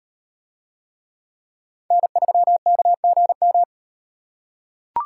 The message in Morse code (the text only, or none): N3KGM